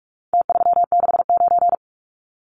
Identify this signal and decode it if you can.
Morse code, heard as a keyed tone, about 44 wpm, 700 Hz. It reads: T369
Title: Numbers - Contest Style - 4-Digit 45wpm